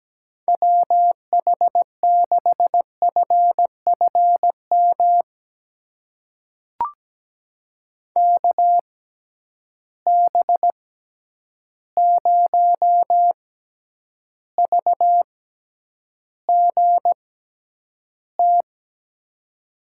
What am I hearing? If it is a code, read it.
Morse code, heard as a keyed tone, about 17 wpm, spaced as for 9 wpm, 700 Hz. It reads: WH6FFM KB0VGT
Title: USA Call Signs - 6 Characters - 6x Character Spacing - 17wpm